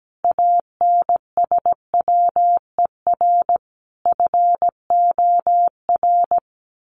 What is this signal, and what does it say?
Morse code, heard as a keyed tone, about 17 wpm, 700 Hz. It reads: ANSWER FOR
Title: Sentences from Top 1400 Words 17wpm